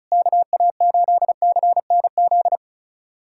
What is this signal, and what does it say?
Morse code, heard as a keyed tone, about 33 wpm, 690 Hz. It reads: KA8CNZ
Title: USA Callsign DE Callsign 35wpm